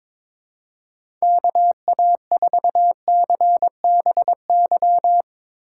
Morse code, heard as a keyed tone, about 22 wpm, 700 Hz. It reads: KA4CBY
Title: USA Call Signs - 6 Characters - 6x Character Spacing - 22wpm